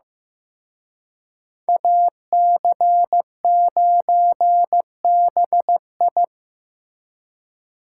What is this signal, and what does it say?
Morse code, heard as a keyed tone, about 18 wpm, 700 Hz. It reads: AC9BI
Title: Speed Racing - USA 2x2 Call Signs 15wpm